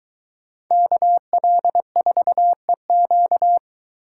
Morse code, heard as a keyed tone, about 21 wpm, 700 Hz. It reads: KL4EQ